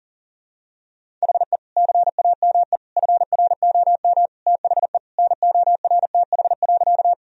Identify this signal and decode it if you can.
Morse code, heard as a keyed tone, about 40 wpm, 700 Hz. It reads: HE CAME FROM THE NORTH.
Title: Sentences from Top 1000 Words - Review Part 1 - 40wpm